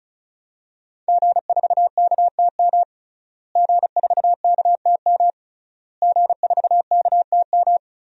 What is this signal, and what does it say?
Morse code, heard as a keyed tone, about 34 wpm, 700 Hz. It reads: G4KTM G4KTM G4KTM
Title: DXCC-100 Call Signs - 5 Characters - Repeated 3x - 35wpm